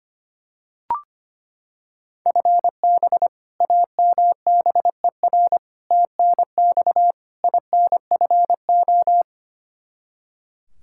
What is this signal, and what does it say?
Morse code, heard as a keyed tone, about 25 wpm, 700 Hz. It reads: FB AMBER TNX INFO
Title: Practice QSO - Exchange 3 - More Info - 25wpm